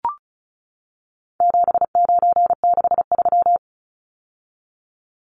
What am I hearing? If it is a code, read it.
Morse code, heard as a keyed tone, about 34 wpm, 700 Hz. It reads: 7963
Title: Numbers - 4-Digit 35wpm